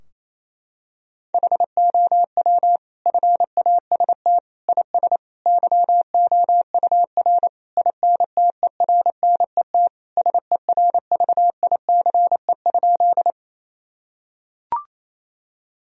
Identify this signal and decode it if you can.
Morse code, heard as a keyed tone, about 28 wpm, 700 Hz. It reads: HOW FAST IS YOUR INTERNET SERVICE?
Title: Sentences from Top 1000 Words - Review Part 4 - 28wpm